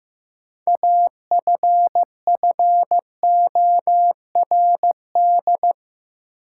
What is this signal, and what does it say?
Morse code, heard as a keyed tone, about 15 wpm, 700 Hz. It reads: AFFORD
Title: Sentences from Top 1100 Words - New Word Drill Alt - 15wpm